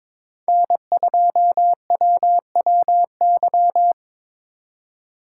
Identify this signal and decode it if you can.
Morse code, heard as a keyed tone, about 26 wpm, 700 Hz. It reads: N2WWY
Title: Speed Racing - USA 1x3 Call Signs 22wpm